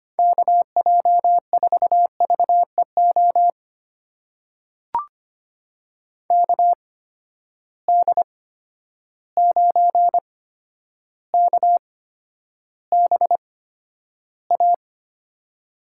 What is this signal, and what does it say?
Morse code, heard as a keyed tone, about 25 wpm, spaced as for 11 wpm, 700 Hz. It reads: KJ4VEO KD9KBA
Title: USA Call Signs - 8x Character Spacing - 25wpm